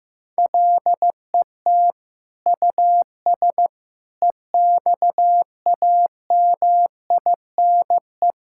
Morse code, heard as a keyed tone, about 15 wpm, 700 Hz. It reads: LET US EXAMINE
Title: Sentences from Top 1600 Words 15wpm